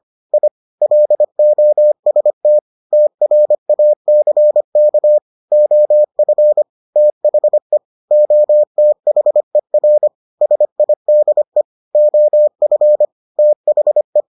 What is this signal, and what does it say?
Morse code, heard as a keyed tone, about 25 wpm, 600 Hz. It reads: I LOST TRACK OF THE OTHER SIDE OF THE